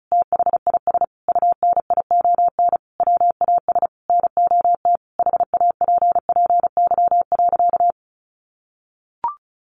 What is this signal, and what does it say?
Morse code, heard as a keyed tone, about 34 wpm, 700 Hz. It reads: THIS UNION WAS NOT HAPPY.
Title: Sentences from Top 1400 Words 35wpm